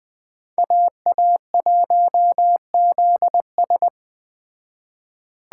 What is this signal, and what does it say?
Morse code, heard as a keyed tone, about 20 wpm, 700 Hz. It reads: AA1ZS